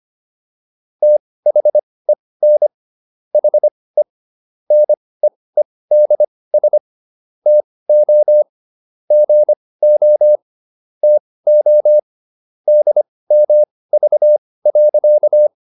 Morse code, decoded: THEN HE NEEDS TO GO TO DMV.

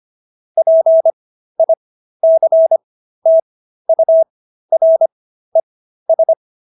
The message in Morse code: PICTURES

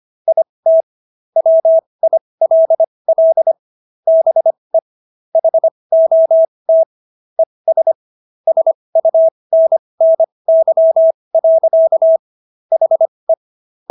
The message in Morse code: IT WILL BE HOT ES SUNNY. HE

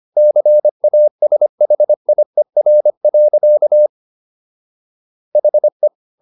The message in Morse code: CASHIER. HE

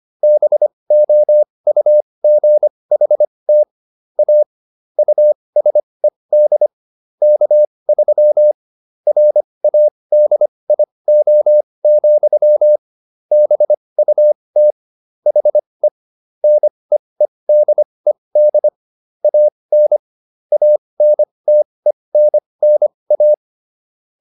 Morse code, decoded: BOUGHT A USED K3 RADIO, BUT HE NEEDED AN ANTENNA

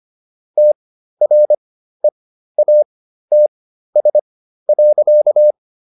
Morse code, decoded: TREATS.

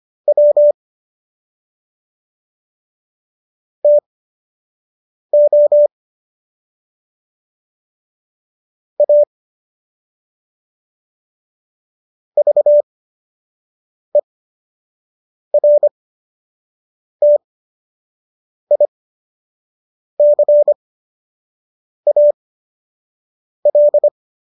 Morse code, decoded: W TO A VERTICAL

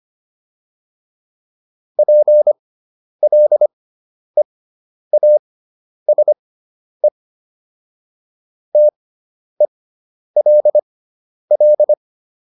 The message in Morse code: PLEASE TELL

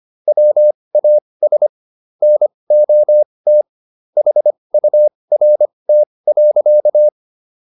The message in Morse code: WAS NOT HURT.

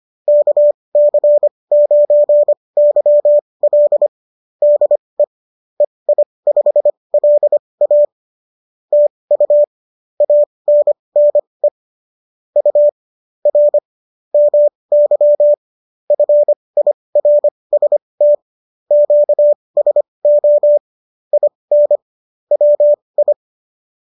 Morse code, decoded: KC9YL DE EI5LA TU ANNE U R MY FIRST QSO IN WI

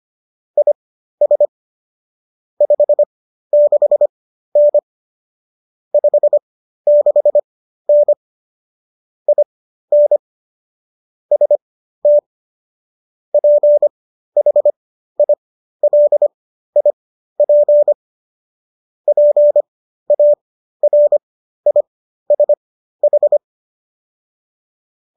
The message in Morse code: IS 56N 56N IN ST PHILIP PARISH

